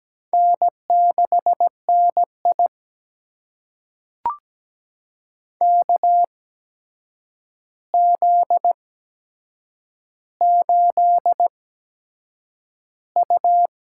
Morse code, heard as a keyed tone, about 17 wpm, spaced as for 7 wpm, 700 Hz. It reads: N6NI KZ8U
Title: USA Call Signs - 4 Characters - 8x Character Spacing - 17wpm